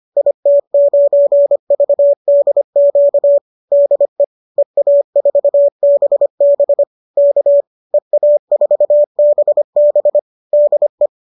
Morse code, heard as a keyed tone, about 25 wpm, 590 Hz. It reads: IT9VDQ DE EA4BB K EA4BB DE